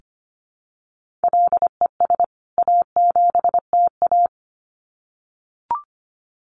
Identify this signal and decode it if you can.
Morse code, heard as a keyed tone, about 25 wpm, 700 Hz. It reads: LES A7TA
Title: CWT Exchange 25wpm